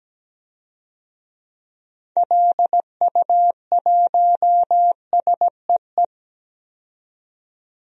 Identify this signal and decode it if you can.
Morse code, heard as a keyed tone, about 17 wpm, 700 Hz. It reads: LU1SEE
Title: DXCC-100 Call Signs - 6 Characters - 17wpm